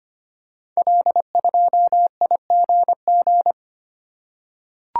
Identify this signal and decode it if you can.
Morse code, heard as a keyed tone, about 25 wpm, 700 Hz. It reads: L2IGG
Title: Code Groups - 5 Letter:Numbers - 25wpm